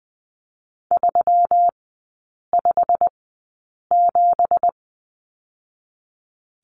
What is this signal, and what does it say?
Morse code, heard as a keyed tone, about 20 wpm, 700 Hz. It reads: 3 5 7
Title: 212 Lesson 34e - Number 8 - Three Characters 20wpm